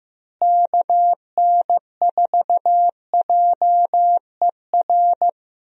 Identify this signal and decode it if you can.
Morse code, heard as a keyed tone, about 15 wpm, 700 Hz. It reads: KN4JER
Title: USA Call Signs - 6 Characters - Repeated 2x - 15wpm